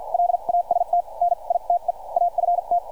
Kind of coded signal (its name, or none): Morse code